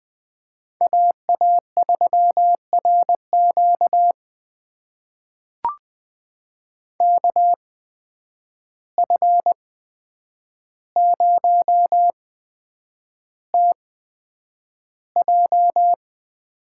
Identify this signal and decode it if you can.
Morse code, heard as a keyed tone, about 21 wpm, spaced as for 9 wpm, 700 Hz. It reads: AA3RQ KF0TJ
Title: USA Call Signs - 5 Characters - 8x Character Spacing - 20wpm